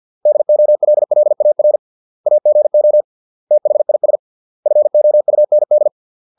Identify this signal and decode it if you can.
Morse code, heard as a keyed tone, about 50 wpm, 600 Hz. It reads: DOLLAR AGO THIS FOUND